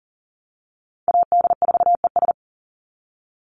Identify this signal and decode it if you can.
Morse code, heard as a keyed tone, about 40 wpm, 700 Hz. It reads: AD4ES
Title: Call Signs Repeated 3x - 40wpm